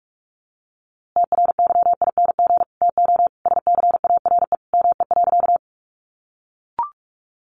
Morse code, heard as a keyed tone, about 44 wpm, 700 Hz. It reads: TRYING TO SCARE ME.
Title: A Sound Of Thunder 45wpm